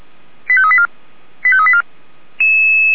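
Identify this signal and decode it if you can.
Five-tone selective call (ZVEI), heard as quick runs of five tones, tones of about 70 ms. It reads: 85384 85384
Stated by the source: ZVEI five-tone selective call 89155 received via SDR.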